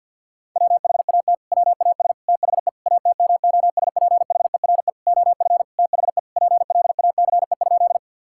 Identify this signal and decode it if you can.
Morse code, heard as a keyed tone, about 50 wpm, 700 Hz. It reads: WHAT WAS THE ATMOSPHERE OF THE PLACE?
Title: Sentences from Top 1600 Words 50wpm